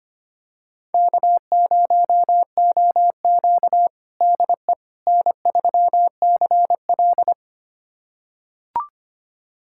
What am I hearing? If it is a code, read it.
Morse code, heard as a keyed tone, about 25 wpm, 700 Hz. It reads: K0OQ DE N3CL
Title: USA Callsign DE Callsign - 4 Characters - 25wpm